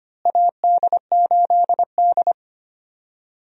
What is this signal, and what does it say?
Morse code, heard as a keyed tone, about 25 wpm, 700 Hz. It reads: AD8D
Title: USA Call Signs - 4 Characters - Repeated 2x - 25wpm